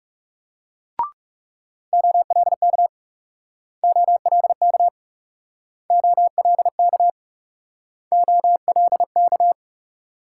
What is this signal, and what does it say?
Morse code, heard as a keyed tone, about 37 wpm, 700 Hz. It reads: OLK OLK OLK OLK
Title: Speed-Racing Multi-letter Phonemes - 30wpm